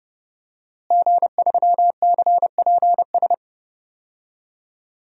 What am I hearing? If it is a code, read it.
Morse code, heard as a keyed tone, about 30 wpm, 700 Hz. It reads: G3CPS
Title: DXCC-100 Call Signs - 5 Characters - Repeated 2x - 30wpm